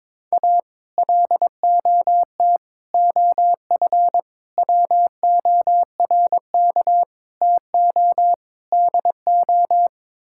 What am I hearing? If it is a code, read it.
Morse code, heard as a keyed tone, about 22 wpm, 700 Hz. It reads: A LOT OF WORK TO DO
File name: practice / 22wpm / Sets of 6 Words - 22wpm